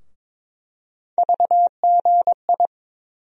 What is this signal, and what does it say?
Morse code, heard as a keyed tone, about 22 wpm, 700 Hz. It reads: VGI
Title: USA Call Sign Postfixes - 3 Letters 22wpm